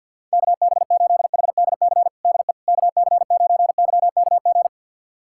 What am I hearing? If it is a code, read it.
Morse code, heard as a keyed tone, about 52 wpm, 700 Hz. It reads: KB8HDC DE KC9YKG
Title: USA Callsign DE Callsign - 6 Characters - 50wpm